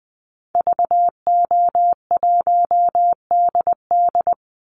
Morse code, heard as a keyed tone, about 20 wpm, 700 Hz. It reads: VO1DD
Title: Call Signs Repeated 2x - 20wpm